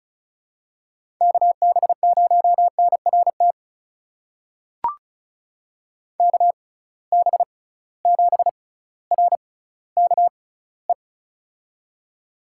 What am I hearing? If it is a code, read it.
Morse code, heard as a keyed tone, about 36 wpm, spaced as for 18 wpm, 700 Hz. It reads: KB0NRT KB7RKE
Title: USA Call Signs - 6 Characters - 6x Character Spacing - 35wpm